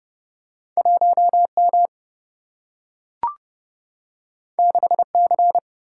Morse code, heard as a keyed tone, about 30 wpm, 700 Hz. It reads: 1 M 6 C
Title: Single-Letter Single-Number 30wpm